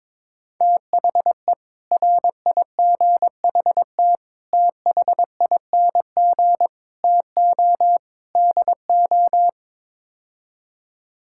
Morse code, heard as a keyed tone, about 21 wpm, 700 Hz. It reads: THE RIGHT THING TO DO
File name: practice / 22wpm / Build A Phrase - 5-Word Phrases - 22wpm